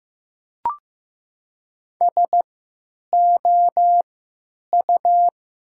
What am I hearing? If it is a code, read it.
Morse code, heard as a keyed tone, about 15 wpm, spaced as for 9 wpm, 700 Hz. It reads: SOU
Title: 3x Character Spacing - Top 100 Words - 15wpm